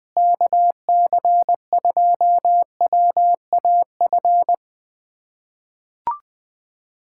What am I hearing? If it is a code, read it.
Morse code, heard as a keyed tone, about 20 wpm, 700 Hz. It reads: KC2WAF
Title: USA Callsign DE Callsign 20wpm